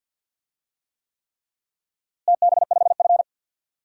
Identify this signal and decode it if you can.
Morse code, heard as a keyed tone, about 52 wpm, 700 Hz. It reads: TB5F
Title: DXCC-100 Call Signs - 4 Characters - Repeated 3x - 50wpm